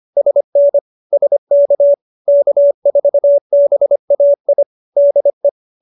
Morse code, heard as a keyed tone, about 25 wpm, 570 Hz. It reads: SN SK K4BAI DE
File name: QSO301 25